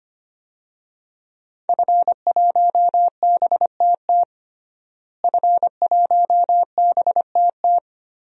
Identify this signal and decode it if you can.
Morse code, heard as a keyed tone, about 25 wpm, 700 Hz. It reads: F1BTT F1BTT
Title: DXCC-100 Call Signs - 5 Characters - Repeated 3x - 25wpm